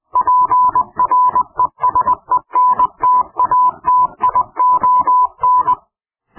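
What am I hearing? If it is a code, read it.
Morse code, heard as a keyed tone, about 20 wpm, 1000 Hz. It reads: PRESENTATION
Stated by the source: Morse code clip offered as an mp3 ring tone, taken from PowerPoint sound clips.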